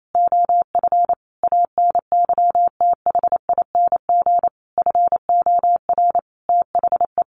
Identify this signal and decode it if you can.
Morse code, heard as a keyed tone, about 28 wpm, 700 Hz. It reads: OF ANYTHING FOR THE
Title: Sentences from Top 1500 Words 28wpm